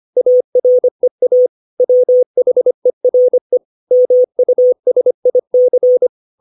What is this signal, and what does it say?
Morse code, heard as a keyed tone, about 25 wpm, 500 Hz. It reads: AREA WHERE MUSIC
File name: 632 words / 632 Words 25WPM